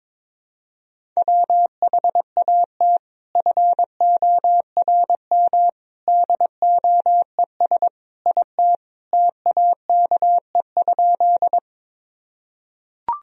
Morse code, heard as a keyed tone, about 21 wpm, 700 Hz. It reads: WHAT FORM DOES IT TAKE?
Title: Sentences from Top 1000 Words - Review Part 1 - 22wpm